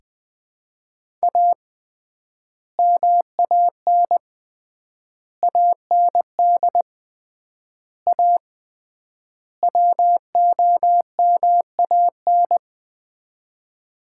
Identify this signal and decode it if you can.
Morse code, heard as a keyed tone, about 20 wpm, 700 Hz. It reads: A MAN AND A WOMAN
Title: 3x Word-Spacing - Sets of 5 Words - 20wpm